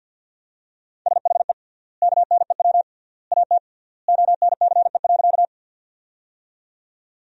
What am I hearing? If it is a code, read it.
Morse code, heard as a keyed tone, about 50 wpm, 700 Hz. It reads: SHE KNEW AT ONCE.
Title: 3x Word-Spacing - Sentences from Top 400 Words - 50wpm